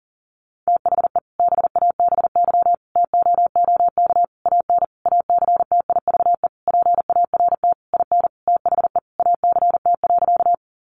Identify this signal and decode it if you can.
Morse code, heard as a keyed tone, about 40 wpm, 700 Hz. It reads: THE BABY TOOK AN ACTIVE PART IN THE ACT.
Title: Sentences from Top 1300 Words 40wpm